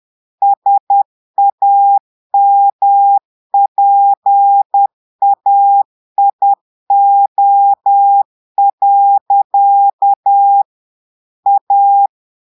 Morse code, heard as a keyed tone, about 10 wpm, 800 Hz. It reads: SAMPAIO. A
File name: Portugais - texte en clair 1 - vitesse - 10 mots minute